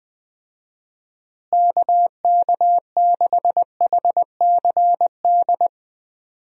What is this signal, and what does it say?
Morse code, heard as a keyed tone, about 21 wpm, 700 Hz. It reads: KK6HCD
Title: USA Call Signs - 6 Characters - Repeated 3x - 20wpm